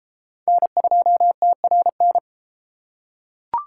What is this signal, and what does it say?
Morse code, heard as a keyed tone, about 31 wpm, 700 Hz. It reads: N2TRN